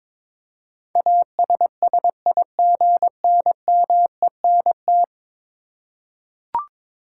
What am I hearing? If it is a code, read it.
Morse code, heard as a keyed tone, about 22 wpm, 700 Hz. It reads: ASSIGNMENT
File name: practice / 22wpm / 10 Letter Words - First 3 Letters Given - 22wpm